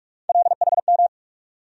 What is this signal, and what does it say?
Morse code, heard as a keyed tone, about 42 wpm, 700 Hz. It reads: LHM
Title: USA Call Sign Postfixes - 3 Letters 45wpm